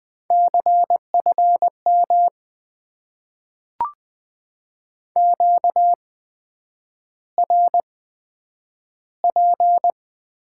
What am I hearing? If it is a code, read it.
Morse code, heard as a keyed tone, about 20 wpm, spaced as for 9 wpm, 700 Hz. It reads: CFM QRP
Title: Top 90 QSO Elements - 8x Character Spacing - 20wpm